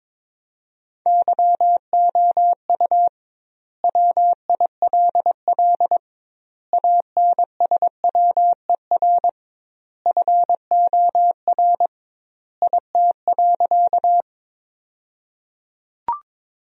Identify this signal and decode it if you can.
Morse code, heard as a keyed tone, about 22 wpm, 700 Hz. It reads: YOU WILL ANSWER FOR IT.
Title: 2x Word-Spacing - Sentences from Top 300 Words - 22wpm